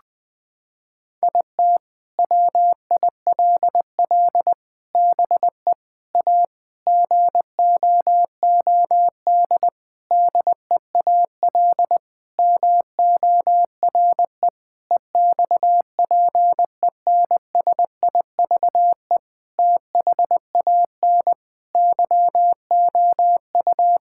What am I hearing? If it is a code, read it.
Morse code, heard as a keyed tone, about 20 wpm, 700 Hz. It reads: IT WILL BE A GOOD DEAL MORE EXPENSIVE THAN YOU
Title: Sentences from Top 1000 Words - Review Part 6 - 20wpm